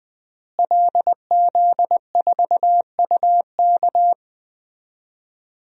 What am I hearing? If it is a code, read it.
Morse code, heard as a keyed tone, about 20 wpm, 700 Hz. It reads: LZ4UK
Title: DXCC-100 Call Signs - 5 Characters - Repeated 3x - 20wpm